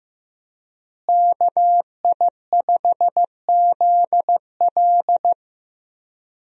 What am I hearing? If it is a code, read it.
Morse code, heard as a keyed tone, about 15 wpm, 700 Hz. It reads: KI5ZL
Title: USA Call Signs - 5 Characters - Repeated 2x - 15wpm